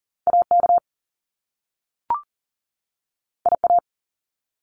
Morse code, heard as a keyed tone, about 39 wpm, 700 Hz. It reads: AK IA
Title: US States 40wpm